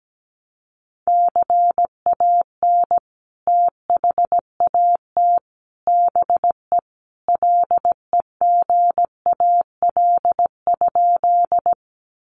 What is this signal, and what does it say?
Morse code, heard as a keyed tone, about 17 wpm, 700 Hz. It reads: CAN THAT BE LEGAL?
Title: Sentences from Top 1200 Words 17wpm